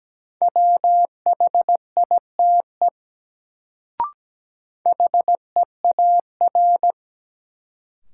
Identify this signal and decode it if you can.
Morse code, heard as a keyed tone, about 17 wpm, 700 Hz. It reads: WHITE HEAR
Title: Sentences from Top 300 Words - New Word Drill - 17wpm